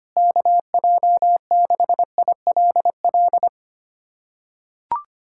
Morse code, heard as a keyed tone, about 25 wpm, 700 Hz. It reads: KJ6ILL
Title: USA Callsign DE Callsign 25wpm